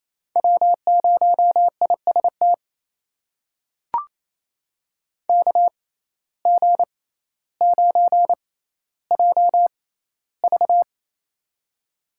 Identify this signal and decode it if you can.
Morse code, heard as a keyed tone, about 28 wpm, spaced as for 14 wpm, 700 Hz. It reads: W0IST KG9JV
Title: USA Call Signs - 5 Characters - 6x Character Spacing - 28wpm